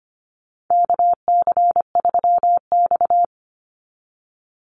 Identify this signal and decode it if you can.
Morse code, heard as a keyed tone, about 25 wpm, 700 Hz. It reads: KC3X
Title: Call Signs Repeated 2x - 25wpm